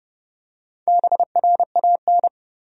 Morse code, heard as a keyed tone, about 30 wpm, 700 Hz. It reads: BRAN
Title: USA Common Names - Max 4 Letters - 30wpm